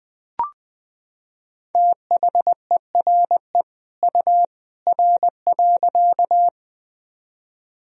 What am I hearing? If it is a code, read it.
Morse code, heard as a keyed tone, about 20 wpm, 700 Hz. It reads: THERE U R.
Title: Sentences from Top 100 Words and Abbreviations 20wpm